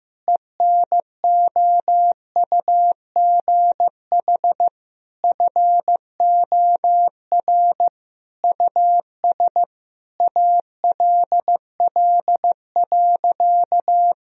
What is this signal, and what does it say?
Morse code, heard as a keyed tone, about 15 wpm, 700 Hz. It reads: ENOUGH FOR US ALL.
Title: Sentences from Top 1000 Words - Review Part 1 - 15wpm